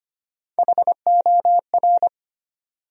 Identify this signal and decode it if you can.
Morse code, heard as a keyed tone, about 25 wpm, 700 Hz. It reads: HOR
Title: USA Call Sign Postfixes - 3 Letters 25wpm